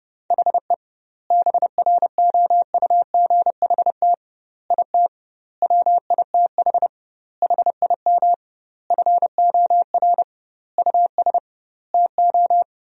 Morse code, heard as a keyed tone, about 30 wpm, 700 Hz. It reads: HE BROUGHT IT WITH HIM FOR US TO
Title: 2x Word-Spacing - Sentences from Top 500 Words - 30wpm